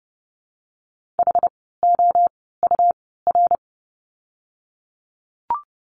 Morse code, heard as a keyed tone, about 30 wpm, spaced as for 17 wpm, 700 Hz. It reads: HOUR
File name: course / 30:10wpm / 108 Lesson 18f - Letter m - Words 30wpm